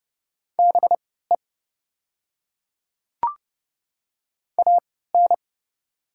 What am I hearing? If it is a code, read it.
Morse code, heard as a keyed tone, about 30 wpm, spaced as for 17 wpm, 700 Hz. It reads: BE AN
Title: 3x Character Spacing - Top 100 Words - 30wpm